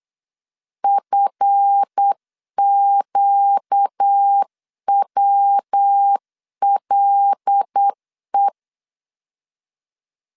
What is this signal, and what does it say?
Morse code, heard as a keyed tone, about 8 wpm, 800 Hz. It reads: FQWLE
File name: LEZIONI / 7 GF9Z / 7 GF 8wpm 3108